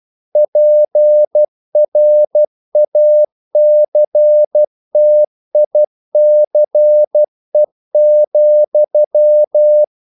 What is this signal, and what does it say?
Morse code, heard as a keyed tone, about 12 wpm, 600 Hz. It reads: PRACTICE,
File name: CW / Practice / EnglishQuotations0112